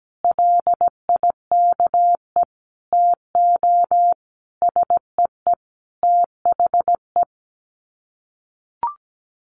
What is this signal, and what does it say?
Morse code, heard as a keyed tone, about 17 wpm, 700 Hz. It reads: LIKE TO SEE THE
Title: Sets of 5 Words Encore2 17wpm